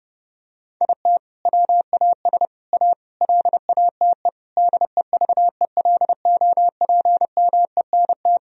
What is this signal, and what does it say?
Morse code, heard as a keyed tone, about 32 wpm, 700 Hz. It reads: IT WAS A LATE DEVELOPMENT
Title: Sentences from Top 1000 Words - Review Part 4 - 30wpm